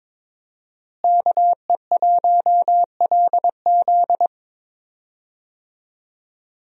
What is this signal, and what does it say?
Morse code, heard as a keyed tone, about 22 wpm, 700 Hz. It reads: KE1LZ
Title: USA Call Signs - 5 Characters - Repeated 2x - 22wpm